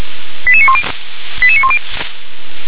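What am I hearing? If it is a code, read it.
Five-tone selective call (ZVEI), heard as quick runs of five tones, tones of about 70 ms. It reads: 88910 88910
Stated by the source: Received on SDR; ZVEI five-tone selective call.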